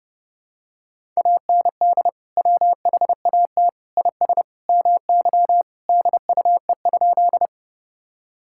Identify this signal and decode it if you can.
Morse code, heard as a keyed tone, about 30 wpm, 700 Hz. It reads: AND WHAT IS MY DUE?
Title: Sentences from Top 1000 Words - Review Part 6 - 30wpm